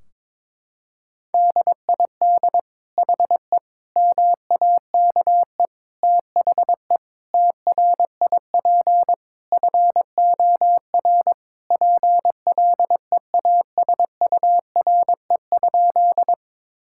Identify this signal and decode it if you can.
Morse code, heard as a keyed tone, about 22 wpm, 700 Hz. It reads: DID HE MAKE THE TRIP FOR PLEASURE?
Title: Sentences from Top 1600 Words 22wpm